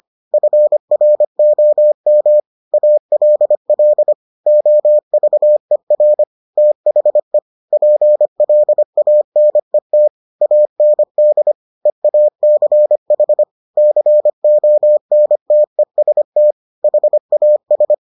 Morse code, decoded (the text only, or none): FROM ALL OVER THE PLANET AND EACH CONTEST HAS